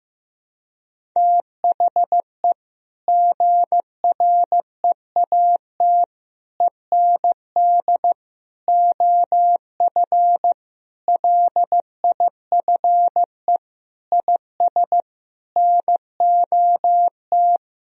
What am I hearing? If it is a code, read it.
Morse code, heard as a keyed tone, about 16 wpm, 700 Hz. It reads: THE GREAT END OF LIFE IS NOT
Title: Sentences from Top 1000 Words - Review Part 4 - 15wpm